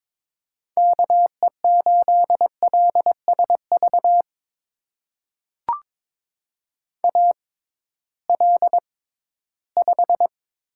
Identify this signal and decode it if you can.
Morse code, heard as a keyed tone, about 21 wpm, spaced as for 14 wpm, 700 Hz. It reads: KE8LSV AL5
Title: USA Call Signs - 6x Character Spacing - 22wpm